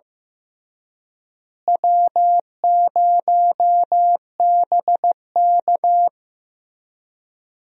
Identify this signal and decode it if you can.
Morse code, heard as a keyed tone, about 15 wpm, 700 Hz. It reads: W0BK
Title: USA Call Signs - 4 Characters - Repeated 2x - 15wpm